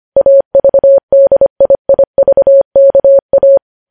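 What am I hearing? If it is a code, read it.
Morse code, heard as a keyed tone, about 25 wpm, 550 Hz. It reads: AVDIIVKA
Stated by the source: Morse code practice of news headlines sent at 25 wpm.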